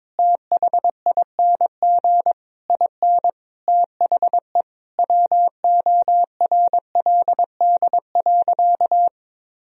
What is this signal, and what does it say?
Morse code, heard as a keyed tone, about 22 wpm, 700 Hz. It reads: THING IN THE WORLD.
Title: Sentences from Top 1000 Words - Review Part 4 - 22wpm